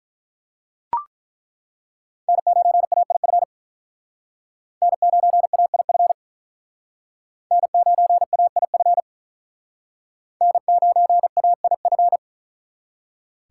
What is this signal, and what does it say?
Morse code, heard as a keyed tone, about 48 wpm, 700 Hz. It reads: N9AIF N9AIF N9AIF N9AIF